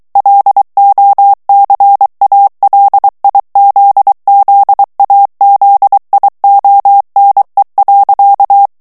Morse code, heard as a keyed tone, about 23 wpm, 800 Hz. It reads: LOCALIZZAZIONE.
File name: morse mp3 / ITALIEN texte clair 1 vitesse 22 mots minute